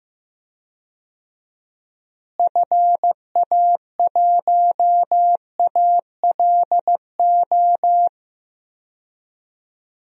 Morse code, heard as a keyed tone, about 15 wpm, 700 Hz. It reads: FA1ALO
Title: DXCC-100 Call Signs - 6 Characters - Repeated 2x - 15wpm